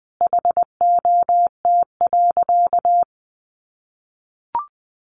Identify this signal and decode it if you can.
Morse code, heard as a keyed tone, about 20 wpm, 700 Hz. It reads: HOT.
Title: Sentences from Top 500 Words 20wpm